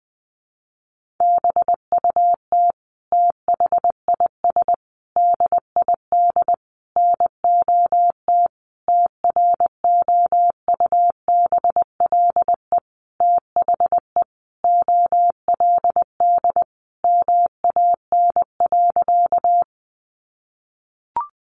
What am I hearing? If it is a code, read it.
Morse code, heard as a keyed tone, about 20 wpm, 700 Hz. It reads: BUT THIS DID NOT TROUBLE THE OLD MAN.
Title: Sentences from Top 1200 Words 20wpm